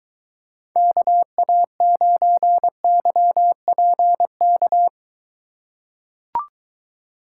Morse code, heard as a keyed tone, about 23 wpm, 700 Hz. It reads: KA9YPK